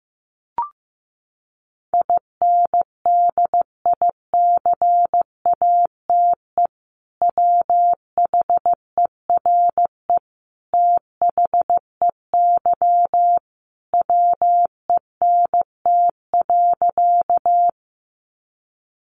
Morse code, heard as a keyed tone, about 15 wpm, 700 Hz. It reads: INDICATE WHERE THEY WENT.
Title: Sentences from Top 1300 Words 15wpm